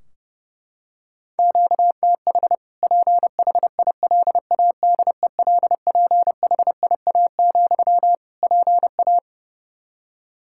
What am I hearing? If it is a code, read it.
Morse code, heard as a keyed tone, about 30 wpm, 700 Hz. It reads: QTH PHILADELPHIA, PA
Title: Practice QSO - Exchange 1 - Beginning - 30wpm